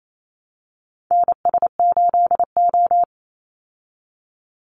Morse code, heard as a keyed tone, about 28 wpm, 700 Hz. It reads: NS8O
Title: Call Signs Repeated 2x - 28wpm